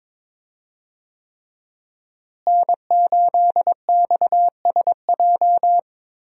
Morse code, heard as a keyed tone, about 22 wpm, 700 Hz. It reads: N8XSJ
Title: USA Call Signs - 5 Characters - Repeated 3x - 22wpm